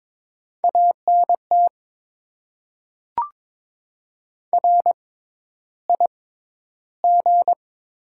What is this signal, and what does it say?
Morse code, heard as a keyed tone, about 22 wpm, spaced as for 11 wpm, 700 Hz. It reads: ANT RIG
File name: Top 20 QSO Elements - 6x Character Spacing - 22wpm